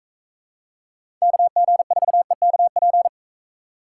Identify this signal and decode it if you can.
Morse code, heard as a keyed tone, about 42 wpm, 700 Hz. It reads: KG4EKP